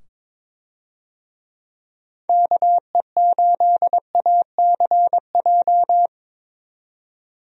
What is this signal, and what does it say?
Morse code, heard as a keyed tone, about 22 wpm, 700 Hz. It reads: KE8ACJ
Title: USA Call Signs - 6 Characters - 22wpm